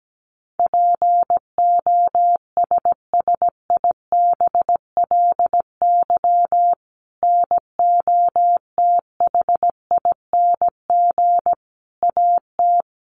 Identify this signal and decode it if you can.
Morse code, heard as a keyed tone, about 17 wpm, 700 Hz. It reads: POSSIBLY NOTHING AT
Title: Sentences from Top 1300 Words 17wpm